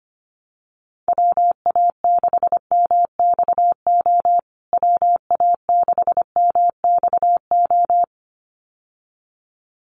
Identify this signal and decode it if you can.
Morse code, heard as a keyed tone, about 25 wpm, 700 Hz. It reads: WA6MXO WA6MXO
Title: USA Call Signs Repeated 2x 25wpm